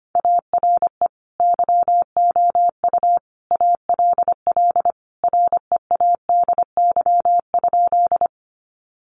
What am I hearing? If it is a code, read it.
Morse code, heard as a keyed tone, about 25 wpm, 700 Hz. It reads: ARE YOU ALL READY?
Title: Sentences from Top 400 Words 25wpm